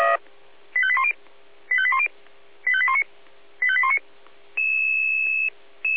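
Five-tone selective call (ZVEI), heard as quick runs of five tones, tones of about 70 ms. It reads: 85810 85810 85819 85819
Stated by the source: Received on SDR.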